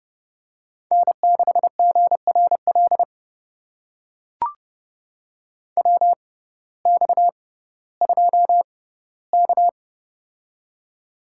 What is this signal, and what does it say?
Morse code, heard as a keyed tone, about 30 wpm, spaced as for 16 wpm, 700 Hz. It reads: N6GRL WX2K
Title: USA Call Signs - 6x Character Spacing - 30wpm